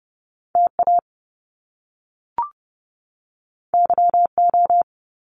Morse code, heard as a keyed tone, about 30 wpm, 700 Hz. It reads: TA YO
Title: Top Two Letter Combinations in Words 30wpm